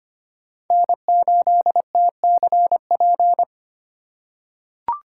Morse code, heard as a keyed tone, about 25 wpm, 700 Hz. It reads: N8TCP
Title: USA Call Signs - 5 Characters - 4x Character Spacing - 25wpm